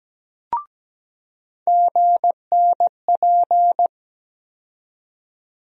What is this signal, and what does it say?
Morse code, heard as a keyed tone, about 17 wpm, 700 Hz. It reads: GNP
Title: USA Call Sign Postfixes - 3 Letters 17wpm